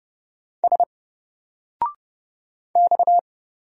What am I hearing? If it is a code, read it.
Morse code, heard as a keyed tone, about 30 wpm, 700 Hz. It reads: S X
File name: Single-Letter-Number-30wpm